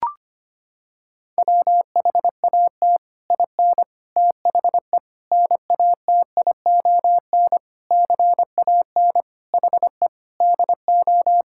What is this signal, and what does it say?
Morse code, heard as a keyed tone, about 25 wpm, 700 Hz. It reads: WHAT IN THE NATION CAN HE DO
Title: Sentences from Top 1000 Words - Review Part 6 - 25wpm